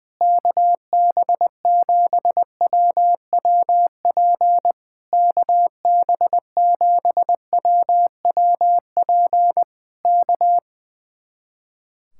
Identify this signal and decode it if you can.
Morse code, heard as a keyed tone, about 20 wpm, 700 Hz. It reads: KB7WWP KB7WWP K
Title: Speed-Racing Common QSO Phrases - 20wpm